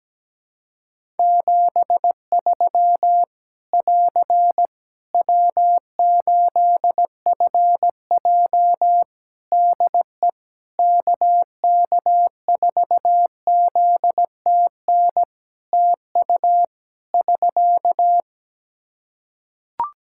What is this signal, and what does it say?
Morse code, heard as a keyed tone, about 17 wpm, 700 Hz. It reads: 73 <AR> W8FJ DE KK4ZTN TU <SK>
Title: Practice QSO - Exchange 4 - Ending - 17wpm